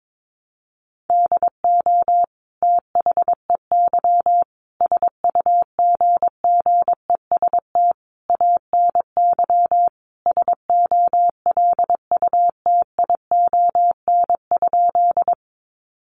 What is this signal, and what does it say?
Morse code, heard as a keyed tone, about 22 wpm, 700 Hz. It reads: DO THEY SUGGEST ANY SOLUTION?
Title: Sentences from Top 1300 Words 22wpm